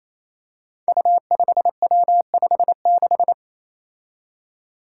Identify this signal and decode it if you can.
Morse code, heard as a keyed tone, about 28 wpm, 700 Hz. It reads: U5W56
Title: Five Challenging Letter-Number Combinations 28wpm